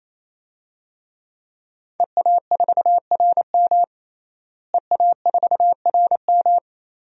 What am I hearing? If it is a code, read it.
Morse code, heard as a keyed tone, about 28 wpm, 700 Hz. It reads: EA4RM EA4RM
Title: DXCC-100 Call Signs - 5 Characters - Repeated 3x - 28wpm